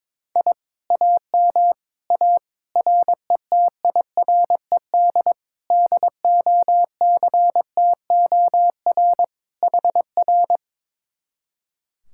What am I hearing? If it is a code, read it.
Morse code, heard as a keyed tone, about 22 wpm, 700 Hz. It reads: I AM A RETIRED DOCTOR HR
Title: Speed-Racing Common QSO Phrases - 22wpm